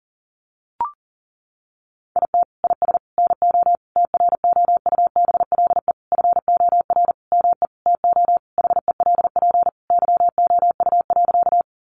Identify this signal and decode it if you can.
Morse code, heard as a keyed tone, about 40 wpm, 700 Hz. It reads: IT IS NO TROUBLE FOR ME TO HELP YOU.
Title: Sentences from Top 1200 Words 40wpm